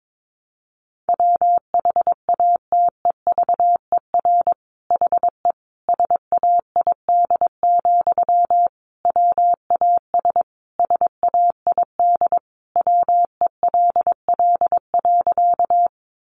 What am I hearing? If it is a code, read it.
Morse code, heard as a keyed tone, about 22 wpm, 700 Hz. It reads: WHATEVER HE SAID, WAS SAID WELL.
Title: Sentences from Top 1100 Words 22wpm